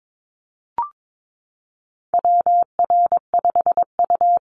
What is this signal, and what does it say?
Morse code, heard as a keyed tone, about 22 wpm, 700 Hz. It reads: WR5U
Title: Call Signs Repeated 2x - 22wpm